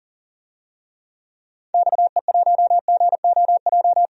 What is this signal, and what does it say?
Morse code, heard as a keyed tone, about 40 wpm, 700 Hz. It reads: XE1GOJ
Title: DXCC-100 Call Signs - 6 Characters - 40wpm